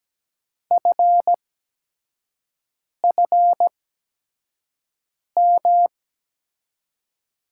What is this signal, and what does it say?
Morse code, heard as a keyed tone, about 17 wpm, spaced as for 5 wpm, 700 Hz. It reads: FFM
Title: USA Call Signs - 6 Characters - 8x Character Spacing - 17wpm